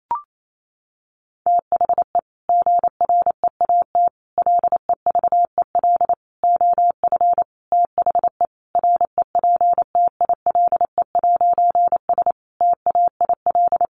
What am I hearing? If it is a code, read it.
Morse code, heard as a keyed tone, about 28 wpm, 700 Hz. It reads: THE GREAT LEVEL OF THE REPTILE'S TAIL
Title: A Sound Of Thunder 28wpm